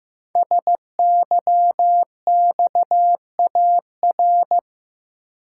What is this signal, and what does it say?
Morse code, heard as a keyed tone, about 15 wpm, 700 Hz. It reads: SYXAR
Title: Code Groups - 5 Letters - 15wpm